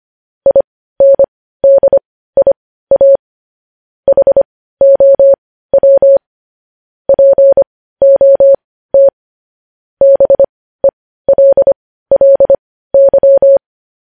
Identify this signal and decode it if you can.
Morse code, heard as a keyed tone, about 25 wpm, spaced as for 15 wpm, 550 Hz. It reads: INDIA HOW POT BELLY
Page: news headlines in morse code at 15 wpm